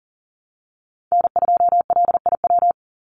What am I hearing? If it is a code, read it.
Morse code, heard as a keyed tone, about 40 wpm, 700 Hz. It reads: N2LIW
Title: USA Call Signs 40wpm